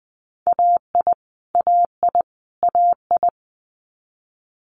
Morse code, heard as a keyed tone, about 20 wpm, 700 Hz. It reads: AI AI AI
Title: Multi-letter Phonemes Repeated 3x 20wpm